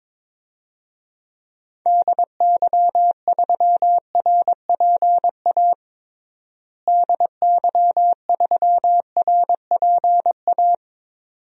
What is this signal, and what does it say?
Morse code, heard as a keyed tone, about 22 wpm, 700 Hz. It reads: DY3RPA DY3RPA
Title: DXCC-100 Call Signs - 6 Characters - Repeated 2x - 22wpm